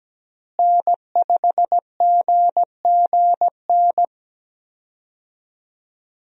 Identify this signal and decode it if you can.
Morse code, heard as a keyed tone, about 17 wpm, 700 Hz. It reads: N5GGN
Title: USA Call Signs - 5 Characters - Repeated 3x - 17wpm